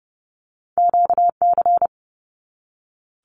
Morse code, heard as a keyed tone, about 30 wpm, 700 Hz. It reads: QC
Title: Canadian Provinces and Territories 30wpm